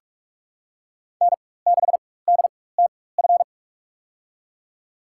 Morse code, heard as a keyed tone, about 44 wpm, spaced as for 21 wpm, 700 Hz. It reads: N6DTF
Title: USA Call Signs - 5 Characters - 4x Character Spacing - 45wpm